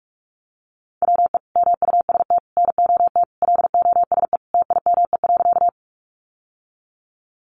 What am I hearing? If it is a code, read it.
Morse code, heard as a keyed tone, about 44 wpm, 700 Hz. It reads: WE MUST NOT LOSE TIME.
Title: Sentences from Top 1100 Words 45wpm